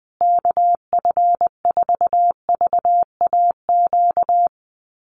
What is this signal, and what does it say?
Morse code, heard as a keyed tone, about 20 wpm, 700 Hz. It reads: KF4VAQ
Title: USA Call Signs Repeated 3x 20wpm